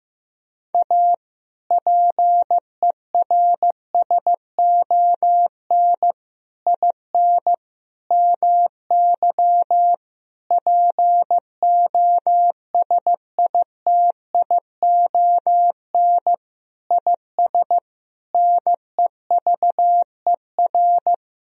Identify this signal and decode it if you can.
Morse code, heard as a keyed tone, about 15 wpm, 700 Hz. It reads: A PERSON IN MY POSITION IS NEVER
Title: Sentences from Top 1000 Words - Review Part 4 - 15wpm